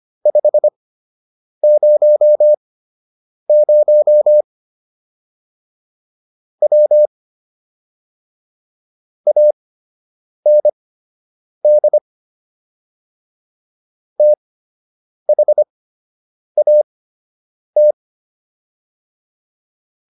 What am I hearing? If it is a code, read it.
Morse code, heard as a keyed tone, about 25 wpm, spaced as for 8 wpm, 600 Hz. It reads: 500 W AND THAT